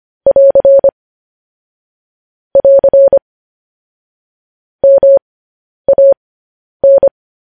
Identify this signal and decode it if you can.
Morse code, heard as a keyed tone, about 25 wpm, spaced as for 10 wpm, 550 Hz. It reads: <AR> <AR> MAN
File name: podcast / News headlines-1x-25x10